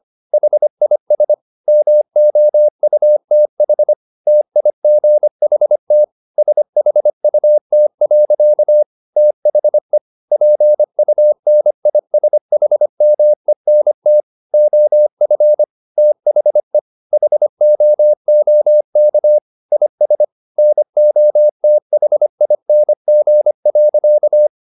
Morse code, heard as a keyed tone, about 25 wpm, 600 Hz. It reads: HIS MOUTH TIGHT SHUT. THE PUNISHMENT OF THE HOOK IS NOTHING.